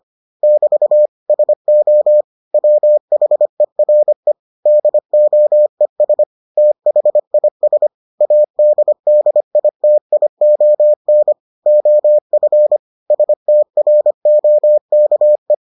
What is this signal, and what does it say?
Morse code, heard as a keyed tone, about 25 wpm, 600 Hz. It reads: <BT> SO WHERE DOES THIS ADDITION OF STROKE